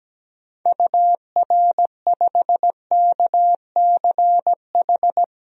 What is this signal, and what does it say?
Morse code, heard as a keyed tone, about 17 wpm, 700 Hz. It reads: UR5KCH
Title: DXCC-100 Call Signs - 6 Characters - 17wpm